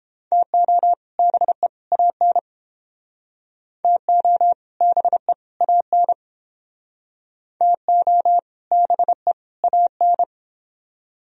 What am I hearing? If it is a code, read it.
Morse code, heard as a keyed tone, about 28 wpm, 700 Hz. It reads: TO BE AN TO BE AN TO BE AN